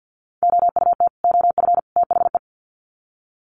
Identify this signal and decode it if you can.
Morse code, heard as a keyed tone, about 50 wpm, 700 Hz. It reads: OUT OF THE
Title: Sets of 3 Words 50wpm